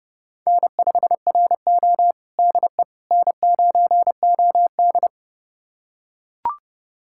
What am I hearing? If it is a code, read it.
Morse code, heard as a keyed tone, about 32 wpm, 700 Hz. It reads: N5RO DE N9OD
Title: USA Callsign DE Callsign - 4 Characters - 30wpm